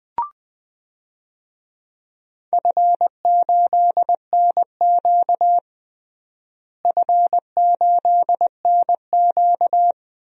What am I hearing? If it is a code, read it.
Morse code, heard as a keyed tone, about 21 wpm, 700 Hz. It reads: F8NQ F8NQ
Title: DXCC-100 Call Signs - 4 Characters - Repeated 3x - 20wpm